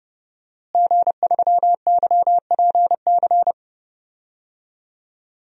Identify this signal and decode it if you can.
Morse code, heard as a keyed tone, about 30 wpm, 700 Hz. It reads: G3YPC
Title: DXCC-100 Call Signs - 5 Characters - Repeated 2x - 30wpm